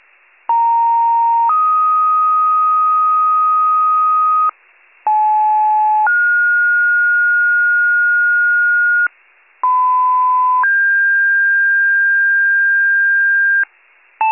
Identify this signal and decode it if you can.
Two-tone page, heard as a one-second tone, then a three-second tone, about 930 then 1250 Hz, about 860 then 1430 Hz, about 1010 then 1600 Hz, with noise. It is heard from 0.5 to 13.6 s.